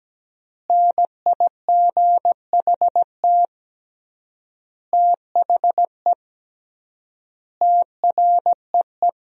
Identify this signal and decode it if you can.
Morse code, heard as a keyed tone, about 17 wpm, 700 Hz. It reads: NIGHT THE TREE
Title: 3x Word-Spacing - Sentences from Top 500 Words - 17wpm